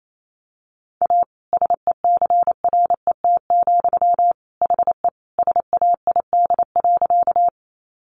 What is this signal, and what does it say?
Morse code, heard as a keyed tone, about 28 wpm, 700 Hz. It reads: A SECRET, HE SAID.
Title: Sentences from Top 1200 Words 28wpm